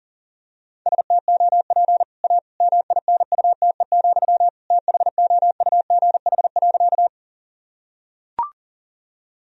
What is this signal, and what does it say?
Morse code, heard as a keyed tone, about 37 wpm, 700 Hz. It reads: STOP A MINUTE, THOUGH.
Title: Sentences from Top 1000 Words - Review Part 2 - 40wpm